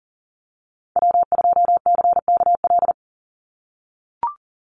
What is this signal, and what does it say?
Morse code, heard as a keyed tone, about 41 wpm, 700 Hz. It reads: W2CKL
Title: NA SOTA Activator Callsigns 40wpm